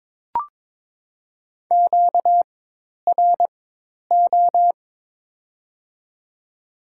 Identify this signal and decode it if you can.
Morse code, heard as a keyed tone, about 22 wpm, spaced as for 10 wpm, 700 Hz. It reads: QRO
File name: Top 90 QSO Elements - 4x Character Spacing - 22wpm